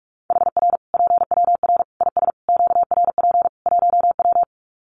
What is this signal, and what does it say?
Morse code, heard as a keyed tone, about 44 wpm, 700 Hz. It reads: HR PWR IS QRP 1W
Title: Common QSO Phrases 45wpm